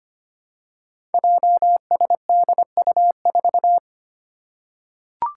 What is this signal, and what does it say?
Morse code, heard as a keyed tone, about 25 wpm, 700 Hz. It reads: JSDU4
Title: Five Challenging Letter-Number Combinations 25wpm